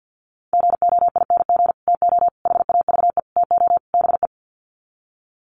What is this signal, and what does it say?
Morse code, heard as a keyed tone, about 50 wpm, 700 Hz. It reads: GOING TO HAVE TO BE
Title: Sets of 5 Words 50wpm